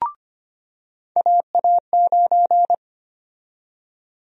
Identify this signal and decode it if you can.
Morse code, heard as a keyed tone, about 25 wpm, 700 Hz. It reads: AA9
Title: USA Call Sign Prefixes - 25wpm